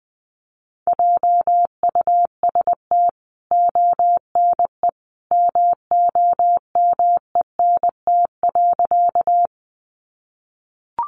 Morse code, heard as a keyed tone, about 20 wpm, 700 Hz. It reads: JUST ONE MOMENT.
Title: Sentences from Top 1000 Words 20wpm